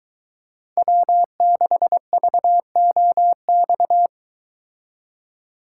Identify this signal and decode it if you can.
Morse code, heard as a keyed tone, about 23 wpm, 700 Hz. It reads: W6VOX